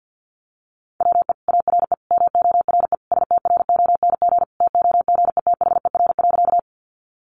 Morse code, heard as a keyed tone, about 50 wpm, 700 Hz. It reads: WE ARE MORE STRONG TOGETHER.
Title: Sentences from Top 500 Words 50wpm